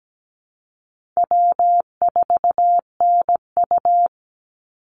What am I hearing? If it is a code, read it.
Morse code, heard as a keyed tone, about 17 wpm, 700 Hz. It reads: W4NU
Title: Easy Call Signs Repeated 3x - 17wpm